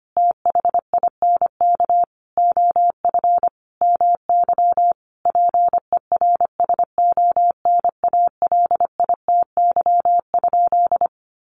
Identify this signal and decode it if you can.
Morse code, heard as a keyed tone, about 25 wpm, 700 Hz. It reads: THINK OF MY PERSONALITY?
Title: Sentences from Top 1400 Words 25wpm